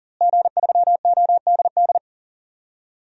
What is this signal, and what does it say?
Morse code, heard as a keyed tone, about 40 wpm, 700 Hz. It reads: G3ODD
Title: DXCC-100 Call Signs - 5 Characters - Repeated 3x - 40wpm